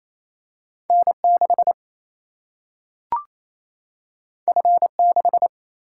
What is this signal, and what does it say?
Morse code, heard as a keyed tone, about 28 wpm, 700 Hz. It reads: N6 F6
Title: Single-Letter Single-Number 28wpm